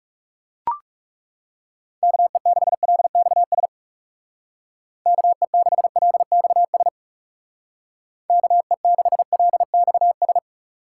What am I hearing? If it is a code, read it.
Morse code, heard as a keyed tone, about 36 wpm, 700 Hz. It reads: KE6LXS KE6LXS KE6LXS
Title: Speed Racing - USA 2x3 Call Signs 30wpm